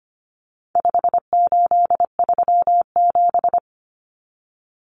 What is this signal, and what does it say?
Morse code, heard as a keyed tone, about 25 wpm, 700 Hz. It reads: 5837
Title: Numbers - 4-Digit 25wpm